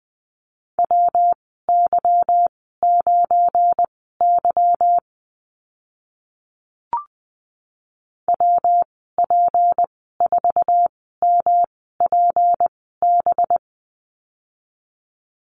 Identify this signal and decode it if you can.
Morse code, heard as a keyed tone, about 20 wpm, spaced as for 15 wpm, 700 Hz. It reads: WY9Y WP4MPB
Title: 207 Lesson 33g - Letter j - Call Signs 20wpm